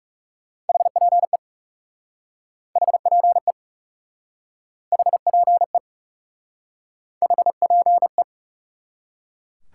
Morse code, heard as a keyed tone, about 38 wpm, 700 Hz. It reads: HPE HPE HPE HPE